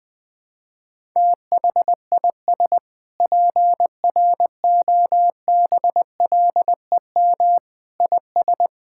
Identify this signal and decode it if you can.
Morse code, heard as a keyed tone, about 20 wpm, 700 Hz. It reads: THIS PROBLEM IS
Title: Sentences with 7-Word Phrases - 20wpm